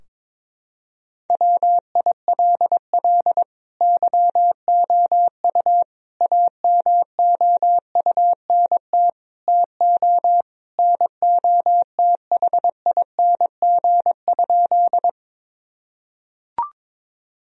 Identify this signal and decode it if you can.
Morse code, heard as a keyed tone, about 22 wpm, 700 Hz. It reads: WILL YOU AMOUNT TO NOTHING?
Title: Sentences from Top 1000 Words - Review Part 4 - 22wpm